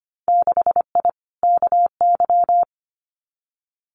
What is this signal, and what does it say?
Morse code, heard as a keyed tone, about 25 wpm, 700 Hz. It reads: 6I KY
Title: Winter Field Day Exchanges Encore 25wpm